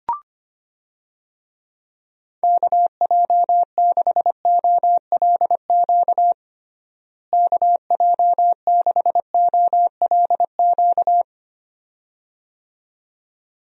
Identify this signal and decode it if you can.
Morse code, heard as a keyed tone, about 25 wpm, 700 Hz. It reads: KJ6OLQ KJ6OLQ
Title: USA Call Signs - 6 Characters - Repeated 2x - 25wpm